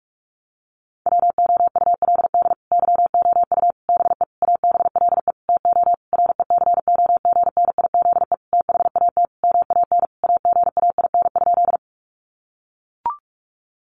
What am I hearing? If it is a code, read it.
Morse code, heard as a keyed tone, about 44 wpm, 700 Hz. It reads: WOULD YOU BE ABLE TO RECOGNIZE THAT MAN AGAIN?
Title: Sentences from Top 1400 Words 45wpm